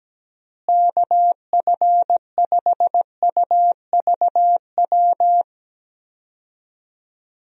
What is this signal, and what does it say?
Morse code, heard as a keyed tone, about 17 wpm, 700 Hz. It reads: KF5UVW
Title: USA Call Signs - 6 Characters - Repeated 2x - 17wpm